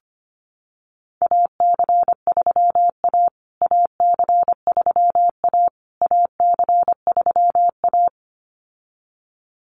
Morse code, heard as a keyed tone, about 25 wpm, 700 Hz. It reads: AC3A AC3A AC3A
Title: USA Call Signs Repeated 3x 25wpm